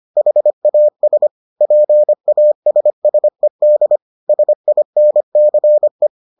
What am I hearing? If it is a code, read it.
Morse code, heard as a keyed tone, about 25 wpm, 600 Hz. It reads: HAS PASSED SINCE